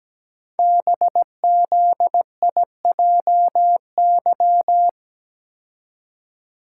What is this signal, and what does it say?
Morse code, heard as a keyed tone, about 17 wpm, 700 Hz. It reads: BZIJY
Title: Code Groups - 5 Letters - 17wpm